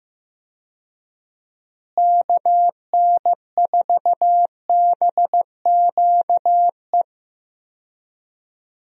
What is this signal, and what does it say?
Morse code, heard as a keyed tone, about 15 wpm, 700 Hz. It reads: KN4BQE
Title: USA Call Signs - 6 Characters - Repeated 2x - 15wpm